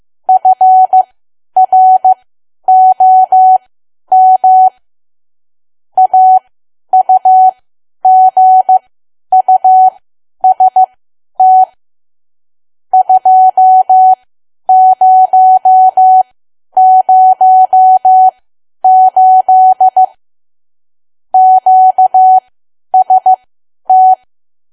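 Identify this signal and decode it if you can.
Morse code, heard as a keyed tone, about 15 wpm, spaced as for 10 wpm, 750 Hz. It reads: FROM AUGUST 2008 QST
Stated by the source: Morse learning practice at 10 wpm.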